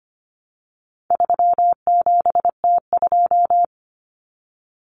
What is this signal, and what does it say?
Morse code, heard as a keyed tone, about 25 wpm, 700 Hz. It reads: 37T2
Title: Common Cut Numbers - 4-Digit 25wpm